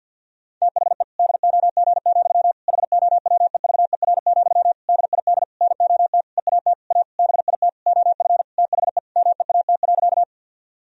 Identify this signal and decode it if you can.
Morse code, heard as a keyed tone, about 50 wpm, 700 Hz. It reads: THE DOG, HOWEVER, DID NOT EAT A BIT OF THE MEAT.
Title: Sentences from Top 1000 Words - Review Part 3 - 50wpm